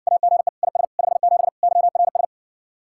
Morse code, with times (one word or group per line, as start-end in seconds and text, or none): 0.1-0.5 AGE
0.6-0.8 IS
1.0-1.5 57
1.6-2.2 YRS